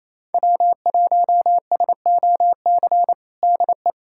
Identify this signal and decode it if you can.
Morse code, heard as a keyed tone, about 28 wpm, 700 Hz. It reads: W1SOC DE
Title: USA Callsign DE Callsign - 5 Characters - 28wpm